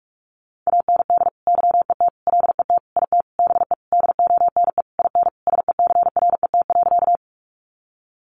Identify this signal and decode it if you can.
Morse code, heard as a keyed tone, about 44 wpm, 700 Hz. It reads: AND YET LET IT BE DONE IN SECRET.
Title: Sentences from Top 1200 Words 45wpm